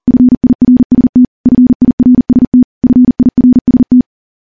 Morse code, heard as a keyed tone, about 43 wpm, 250 Hz. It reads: FIRST FIRST FIRST
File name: morse / morse-training-40WPM